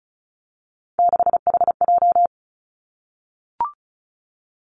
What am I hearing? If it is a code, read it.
Morse code, heard as a keyed tone, about 36 wpm, 700 Hz. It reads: 6HJ
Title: Three Challenging Letter-Number Combinations 35wpm